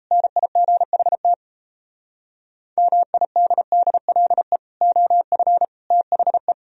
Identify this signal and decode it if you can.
Morse code, heard as a keyed tone, about 35 wpm, 690 Hz. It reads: NIGHT MIDDLE OF THE